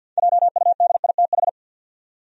Morse code, heard as a keyed tone, about 52 wpm, 700 Hz. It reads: JUDITH
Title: USA Common Names - 50wpm